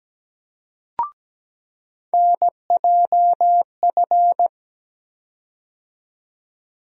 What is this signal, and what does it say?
Morse code, heard as a keyed tone, about 17 wpm, 700 Hz. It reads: NJF
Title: USA Call Sign Postfixes - 3 Letters 17wpm